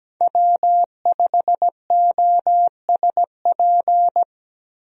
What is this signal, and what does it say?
Morse code, heard as a keyed tone, about 17 wpm, 700 Hz. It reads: W5OSP
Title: USA Call Signs - 5 Characters - Repeated 3x - 17wpm